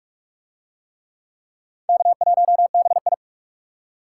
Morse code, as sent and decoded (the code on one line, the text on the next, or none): -.- .---- -... ..
K1BI